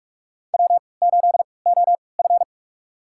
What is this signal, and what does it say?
Morse code, heard as a keyed tone, about 46 wpm, spaced as for 25 wpm, 690 Hz. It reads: W8OF
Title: USA Call Signs - 4 Characters - 3x Character Spacing - 45wpm